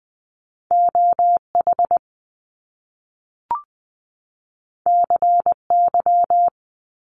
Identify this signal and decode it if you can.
Morse code, heard as a keyed tone, about 20 wpm, 700 Hz. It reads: OH CY
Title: Top Two Letter Combinations in Words 20wpm